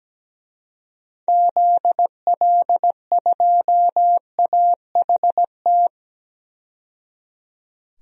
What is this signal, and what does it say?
Morse code, heard as a keyed tone, about 17 wpm, 700 Hz. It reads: ZL2AHT
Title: DXCC-100 Call Signs - 6 Characters - 17wpm